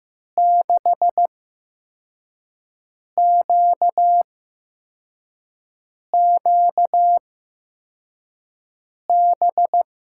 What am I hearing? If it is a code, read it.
Morse code, heard as a keyed tone, about 15 wpm, spaced as for 4 wpm, 700 Hz. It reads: 6QQB
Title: USA Call Signs - 6 Characters - 8x Character Spacing - 15wpm